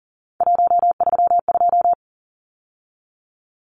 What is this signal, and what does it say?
Morse code, heard as a keyed tone, about 40 wpm, 700 Hz. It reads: 132
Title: Numbers - 3-Digit 40wpm